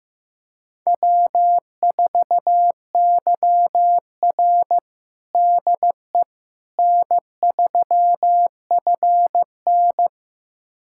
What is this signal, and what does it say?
Morse code, heard as a keyed tone, about 15 wpm, 700 Hz. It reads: W4YR DE N3FN
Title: USA Callsign DE Callsign - 4 Characters - 15wpm